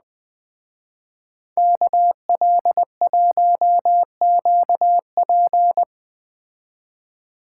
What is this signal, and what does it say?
Morse code, heard as a keyed tone, about 20 wpm, 700 Hz. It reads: KL1QP
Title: Speed Racing - USA 2x2 Call Signs 17wpm